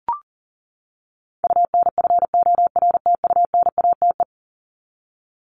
Morse code, heard as a keyed tone, about 40 wpm, 700 Hz. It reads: UNFORTUNATE
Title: LY Suffix - Top 100 Words with Base Words 40wpm